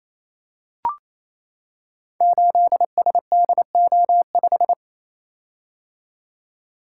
Morse code, heard as a keyed tone, about 29 wpm, 700 Hz. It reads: 8SDO5
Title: Code Groups - 5 Letter:Numbers - 28wpm